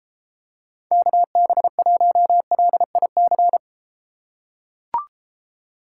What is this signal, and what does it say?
Morse code, heard as a keyed tone, about 33 wpm, 700 Hz. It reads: KB1LIC